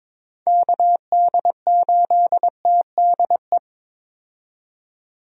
Morse code, heard as a keyed tone, about 22 wpm, 700 Hz. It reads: KD8TDE
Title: USA Call Signs - 6 Characters - 22wpm